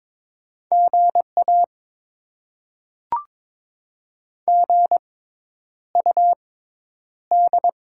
Morse code, heard as a keyed tone, about 22 wpm, spaced as for 10 wpm, 700 Hz. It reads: GA GUD
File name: Top 20 QSO Elements - 6x Character Spacing - 22wpm